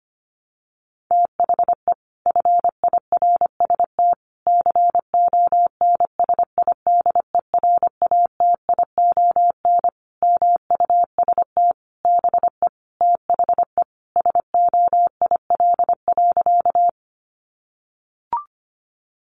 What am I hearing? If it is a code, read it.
Morse code, heard as a keyed tone, about 25 wpm, 700 Hz. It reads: THE FIRST CONSIDERATION MUST BE THE SOIL.
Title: Sentences from Top 1500 Words 25wpm